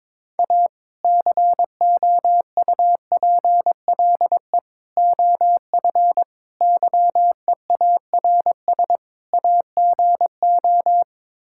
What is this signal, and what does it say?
Morse code, heard as a keyed tone, about 22 wpm, 700 Hz. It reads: A COUPLE OF YEARS AGO
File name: practice / 22wpm / Build A Phrase - 5-Word Phrases - 22wpm